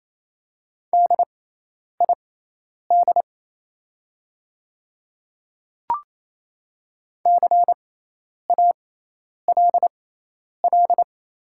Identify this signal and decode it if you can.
Morse code, heard as a keyed tone, about 27 wpm, spaced as for 10 wpm, 700 Hz. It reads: DID CALL
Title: 6x Character Spacing - Top 100 Words - 28wpm